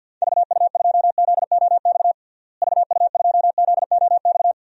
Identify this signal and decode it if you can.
Morse code, heard as a keyed tone, about 53 wpm, 700 Hz. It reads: VU2ZOX VU2ZOX
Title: DXCC-100 Call Signs - 6 Characters - Repeated 2x - 50wpm